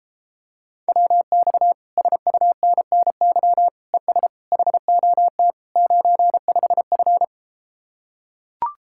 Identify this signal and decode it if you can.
Morse code, heard as a keyed tone, about 33 wpm, 700 Hz. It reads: WX SUNNY ES HOT 95F